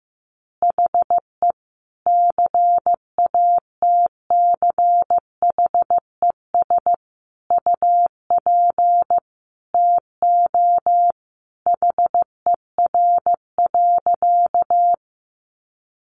Morse code, heard as a keyed tone, about 15 wpm, 700 Hz. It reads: HE CATCHES UP TO HER.
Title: Additional Easy Phrases 15wpm